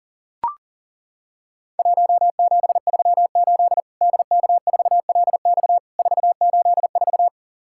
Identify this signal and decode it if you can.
Morse code, heard as a keyed tone, about 41 wpm, 700 Hz. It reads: 1738 DK4LX 484
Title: Worked All Europe DX Contest - QTC Exchange - 40wpm